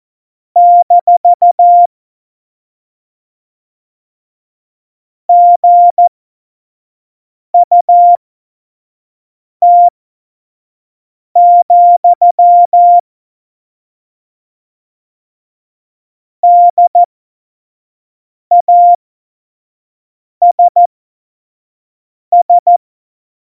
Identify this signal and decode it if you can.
Morse code, heard as a keyed tone, about 14 wpm, spaced as for 5 wpm, 700 Hz. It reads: - GUT, DASS
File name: Morsekurs / Witze / Witz 13. August 2019, 25 F70